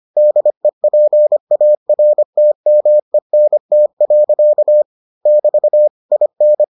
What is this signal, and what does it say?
Morse code, heard as a keyed tone, about 25 wpm, 600 Hz. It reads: DEPARTMENT. <BT> IN